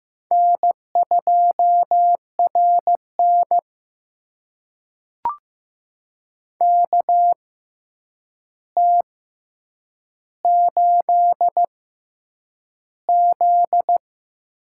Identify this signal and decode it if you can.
Morse code, heard as a keyed tone, about 15 wpm, spaced as for 8 wpm, 700 Hz. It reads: N2RN KT8Z
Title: USA Call Signs - 4 Characters - 6x Character Spacing - 15wpm